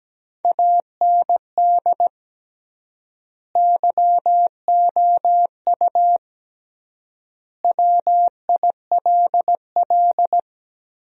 Morse code, heard as a keyed tone, about 17 wpm, 700 Hz. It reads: AND YOU WILL
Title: 3x Word-Spacing - Sentences from Top 400 Words - 17wpm